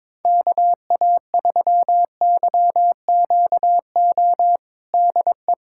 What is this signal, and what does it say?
Morse code, heard as a keyed tone, about 22 wpm, 700 Hz. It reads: KA3YQO DE
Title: USA Callsign DE Callsign - 6 Characters - 22wpm